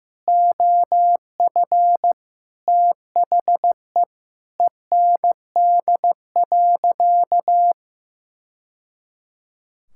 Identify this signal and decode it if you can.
Morse code, heard as a keyed tone, about 15 wpm, 700 Hz. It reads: OF THE END.
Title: Sentences from Top 1000 Words - Review Part 5 - 15wpm